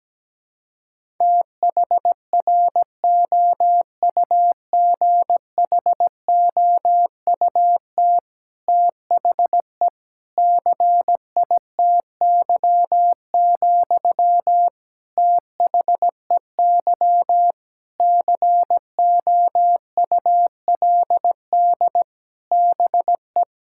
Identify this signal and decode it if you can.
Morse code, heard as a keyed tone, about 17 wpm, 700 Hz. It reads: THROUGHOUT THE CITY, THEY COULD BE
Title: Sentences from Top 1000 Words - Review Part 5 - 17wpm